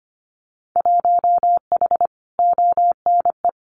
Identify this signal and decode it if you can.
Morse code, heard as a keyed tone, about 25 wpm, 700 Hz. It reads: 1H ONE
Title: Winter Field Day Exchanges 25wpm